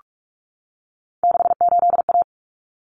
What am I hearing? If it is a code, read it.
Morse code, heard as a keyed tone, about 44 wpm, 700 Hz. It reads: 68A
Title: Common Cut Numbers - 3-Digit 45wpm